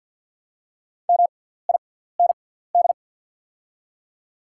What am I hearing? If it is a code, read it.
Morse code, heard as a keyed tone, about 50 wpm, spaced as for 17 wpm, 700 Hz. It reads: MIND
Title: 6x Character Spacing - Top 500 Words - 50wpm